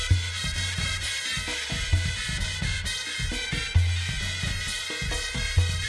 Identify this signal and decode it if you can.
Emergency Alert System (SAME) tones, heard as harsh buzzing bursts. It is heard from 0.9 to 5.0 s.